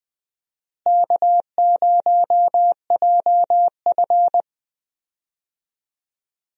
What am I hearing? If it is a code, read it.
Morse code, heard as a keyed tone, about 20 wpm, 700 Hz. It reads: K0JF
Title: USA Call Signs - 4 Characters - Repeated 2x - 20wpm